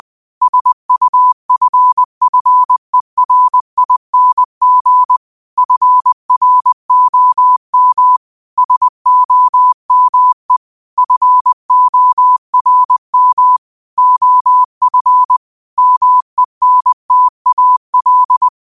Morse code, heard as a keyed tone, about 20 wpm, 1000 Hz. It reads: SUFFERING FROM SOME FORM OF MENTAL